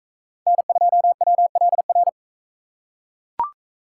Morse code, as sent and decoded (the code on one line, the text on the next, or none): -. ..--- .-- .-.. .-.
N2WLR